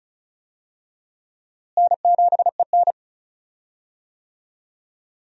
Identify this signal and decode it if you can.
Morse code, heard as a keyed tone, about 36 wpm, 700 Hz. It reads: N7EN
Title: USA Call Signs - 4 Characters - Repeated 3x - 35wpm